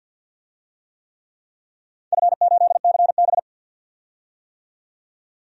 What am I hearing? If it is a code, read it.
Morse code, heard as a keyed tone, about 50 wpm, 700 Hz. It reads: F8CB
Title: DXCC-100 Call Signs - 4 Characters - 50wpm